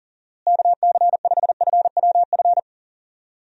Morse code, heard as a keyed tone, about 37 wpm, 700 Hz. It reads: KC5FWF
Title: USA Call Signs - 6 Characters - Repeated 2x - 40wpm